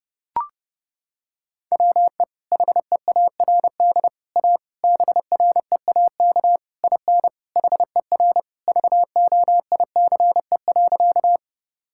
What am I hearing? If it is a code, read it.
Morse code, heard as a keyed tone, about 29 wpm, 700 Hz. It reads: WE HEARD A BREAK IN HER VOICE.
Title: Sentences from Top 1000 Words - Review Part 6 - 30wpm